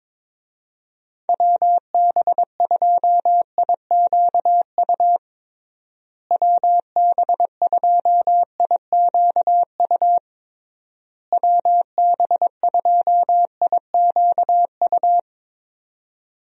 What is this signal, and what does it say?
Morse code, heard as a keyed tone, about 22 wpm, 700 Hz. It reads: WB2IQU WB2IQU WB2IQU
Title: USA Call Signs - 6 Characters - Repeated 3x - 22wpm